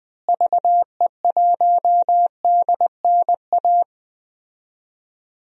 Morse code, heard as a keyed tone, about 20 wpm, 700 Hz. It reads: VE1DNA
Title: DXCC-100 Call Signs - 6 Characters - Repeated 3x - 20wpm